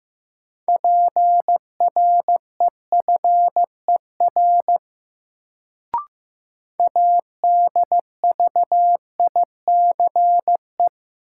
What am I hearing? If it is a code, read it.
Morse code, heard as a keyed tone, about 15 wpm, 700 Hz. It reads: PREFER ADVICE
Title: Sentences from Top 1100 Words - New Word Drill Alt - 15wpm